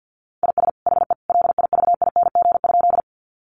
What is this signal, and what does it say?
Morse code, heard as a keyed tone, about 50 wpm, 700 Hz. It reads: IS HE LIVING?
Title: Sentences from Top 1100 Words 50wpm